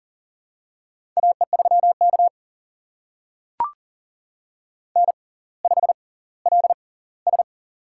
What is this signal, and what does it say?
Morse code, heard as a keyed tone, about 39 wpm, spaced as for 20 wpm, 690 Hz. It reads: AE3K N5LS
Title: USA Call Signs - 4 Characters - 6x Character Spacing - 40wpm